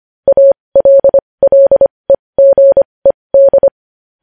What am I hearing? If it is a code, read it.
Morse code, heard as a keyed tone, about 25 wpm, spaced as for 20 wpm, 550 Hz. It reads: ALLEGED
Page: news-headlines-in-morse-code-at-20-wpm